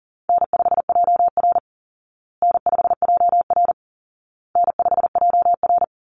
Morse code, heard as a keyed tone, about 40 wpm, 700 Hz. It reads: N5JR N5JR N5JR
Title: Easy Call Signs Repeated 3x - 40wpm